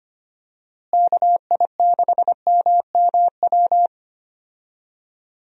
Morse code, heard as a keyed tone, about 25 wpm, 700 Hz. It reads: KI6MMW
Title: USA Call Signs - 6 Characters - Repeated 2x - 25wpm